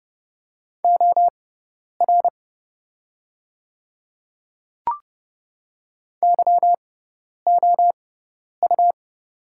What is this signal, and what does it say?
Morse code, heard as a keyed tone, about 30 wpm, spaced as for 10 wpm, 690 Hz. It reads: OR YOU
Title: 6x Character Spacing - Top 200 Words - 30wpm